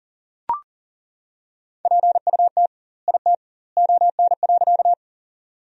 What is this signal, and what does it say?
Morse code, heard as a keyed tone, about 38 wpm, 700 Hz. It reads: PUT IT ON.
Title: 2x Word-Spacing - Sentences from Top 200 Words - 40wpm